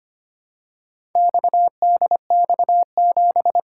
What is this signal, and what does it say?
Morse code, heard as a keyed tone, about 25 wpm, 700 Hz. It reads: XDX7
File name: practice / 25wpm / Four Letter-Number Combinations - 25wpm